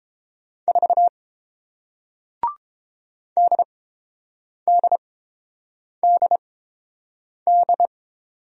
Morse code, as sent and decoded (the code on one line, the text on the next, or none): ....- / -.. -.. -.. -..
4 DDDD